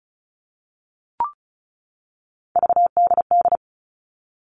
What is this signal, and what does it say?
Morse code, heard as a keyed tone, about 34 wpm, 700 Hz. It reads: VDD
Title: All Cut Numbers - 3-Digit 35wpm